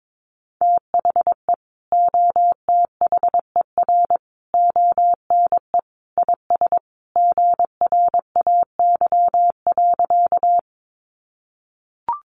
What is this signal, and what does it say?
Morse code, heard as a keyed tone, about 22 wpm, 700 Hz. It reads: THE OTHER ONE IS GRAY.
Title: Additional Easy Phrases 22wpm